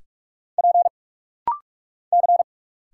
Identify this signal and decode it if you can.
Morse code, heard as a keyed tone, about 46 wpm, 700 Hz. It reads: P C
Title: Single Letters - Rapid-Fire 45wpm